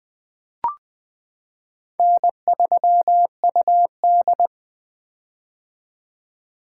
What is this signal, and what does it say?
Morse code, heard as a keyed tone, about 20 wpm, 700 Hz. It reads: N3UD
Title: 2023 POTA Activator Call Signs - 20wpm